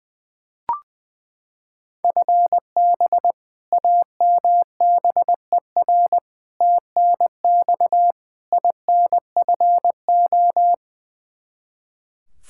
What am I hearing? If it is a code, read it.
Morse code, heard as a keyed tone, about 21 wpm, 700 Hz. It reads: FB AMBER TNX INFO
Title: Practice QSO - Exchange 3 - More Info - 20wpm